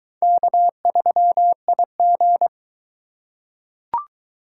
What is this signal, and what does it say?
Morse code, heard as a keyed tone, about 22 wpm, 700 Hz. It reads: K3IG